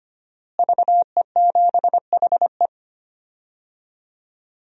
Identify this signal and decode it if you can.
Morse code, heard as a keyed tone, about 25 wpm, 700 Hz. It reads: VE7HE
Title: DXCC-100 Call Signs - 5 Characters - Repeated 2x - 25wpm